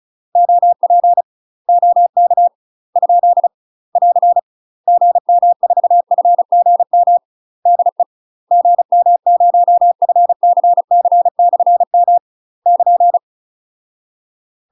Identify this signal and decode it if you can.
Morse code, heard as a keyed tone, about 36 wpm, 700 Hz. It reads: OP OK ? <AR> GM4FGM DE GM0FCC/M <KN>